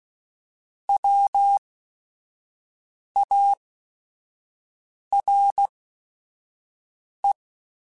Morse code, heard as a keyed tone, about 16 wpm, spaced as for 5 wpm, 800 Hz. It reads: WARE